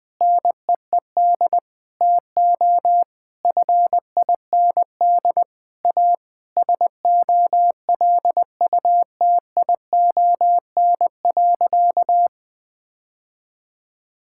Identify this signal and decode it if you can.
Morse code, heard as a keyed tone, about 21 wpm, 700 Hz. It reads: NEED TO FIND A SOLUTION.
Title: Sentences with 3-Word Phrases - Spoken Beginning - 20wpm